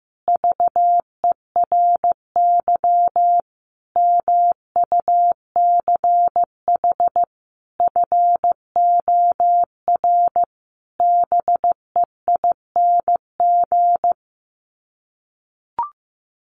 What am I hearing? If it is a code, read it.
Morse code, heard as a keyed tone, about 15 wpm, 700 Hz. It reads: VERY MUCH FOR BEING
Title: Sets of 5 Words Encore 15wpm